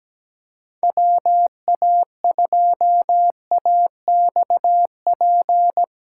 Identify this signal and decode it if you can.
Morse code, heard as a keyed tone, about 18 wpm, 700 Hz. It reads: WA2AXP
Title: Speed Racing - USA Call Signs 17wpm